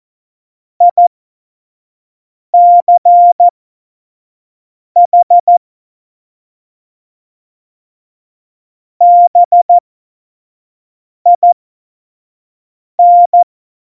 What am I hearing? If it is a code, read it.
Morse code, heard as a keyed tone, about 14 wpm, spaced as for 5 wpm, 700 Hz. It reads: ICH BIN
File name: Morsekurs / Witze / Witz 26. Dezember 2017, 25 F70